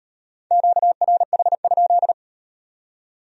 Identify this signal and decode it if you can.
Morse code, heard as a keyed tone, about 36 wpm, 700 Hz. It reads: QRH?